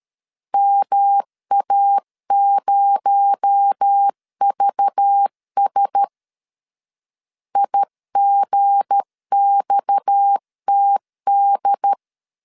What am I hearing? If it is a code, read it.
Morse code, heard as a keyed tone, about 13 wpm, 800 Hz. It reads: MA0VS IGXTD
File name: LEZIONI / 7 GF9Z / 7 G 12wpm 2055